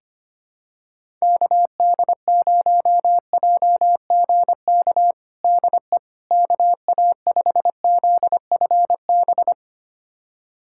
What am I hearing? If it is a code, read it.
Morse code, heard as a keyed tone, about 25 wpm, 700 Hz. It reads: KD0JGK DE KA5ZFB
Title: USA Callsign DE Callsign - 6 Characters - 25wpm